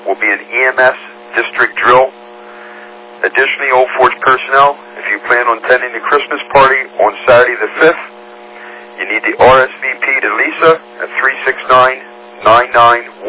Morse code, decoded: none

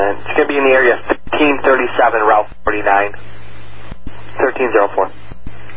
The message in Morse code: none

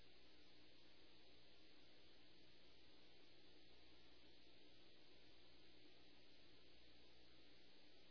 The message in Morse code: none

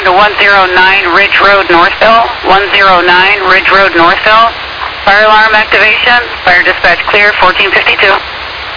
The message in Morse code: none

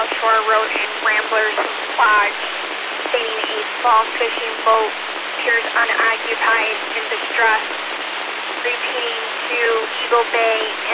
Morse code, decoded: none

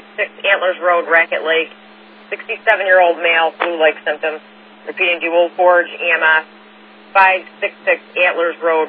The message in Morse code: none

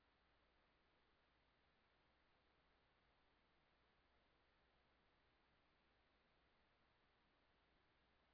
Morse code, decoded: none

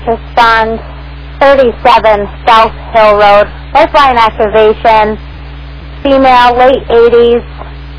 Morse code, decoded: none

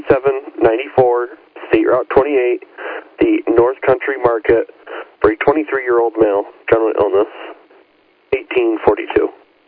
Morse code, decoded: none

